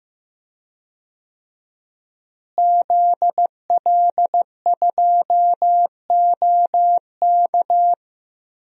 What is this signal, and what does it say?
Morse code, heard as a keyed tone, about 15 wpm, 700 Hz. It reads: ZL2OK
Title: DXCC-100 Call Signs - 5 Characters - Repeated 3x - 15wpm